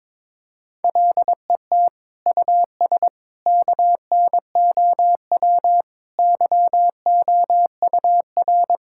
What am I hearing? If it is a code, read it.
Morse code, heard as a keyed tone, about 22 wpm, 700 Hz. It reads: LET US KNOW YOUR
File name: Sentences with 7-Word Phrases - 22wpm